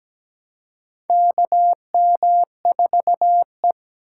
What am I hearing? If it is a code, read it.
Morse code, heard as a keyed tone, about 17 wpm, 700 Hz. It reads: KM4E
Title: USA Call Signs - 4 Characters - 3x Character Spacing - 17wpm